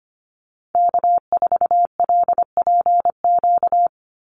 Morse code, heard as a keyed tone, about 25 wpm, 700 Hz. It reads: K4LPQ
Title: NA SOTA Activator Callsigns 25wpm